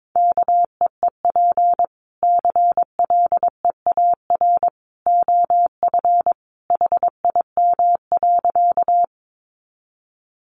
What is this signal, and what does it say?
Morse code, heard as a keyed tone, about 22 wpm, 700 Hz. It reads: KEEP CLEAR OF HIM.
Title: Sentences from Top 500 Words 22wpm